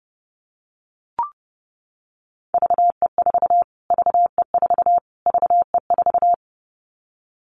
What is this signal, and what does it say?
Morse code, heard as a keyed tone, about 30 wpm, 690 Hz. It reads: VE4 VE4 VE4
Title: Callsign Prefixes for Top 25 DXCC - Repeated 3x - 30wpm